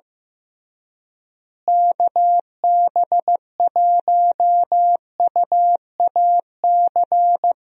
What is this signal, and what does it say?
Morse code, heard as a keyed tone, about 14 wpm, 700 Hz. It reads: KB1UAC
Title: USA Call Signs - 6 Characters - Repeated 3x - 15wpm